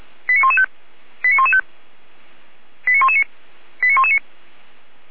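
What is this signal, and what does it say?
Five-tone selective call (ZVEI), heard as quick runs of five tones, tones of about 80 ms. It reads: 89115 89115 89119 89119
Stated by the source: ZVEI five-tone selective call received on an SDR.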